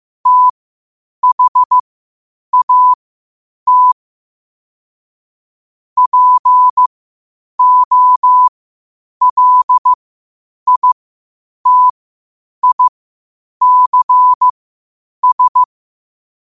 Morse code, heard as a keyed tone, about 15 wpm, spaced as for 8 wpm, 1000 Hz. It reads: THAT POLITICS
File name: qotdfiles / 8wpm